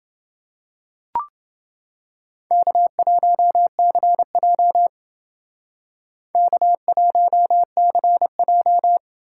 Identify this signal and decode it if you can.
Morse code, heard as a keyed tone, about 26 wpm, 700 Hz. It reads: K1CJ K1CJ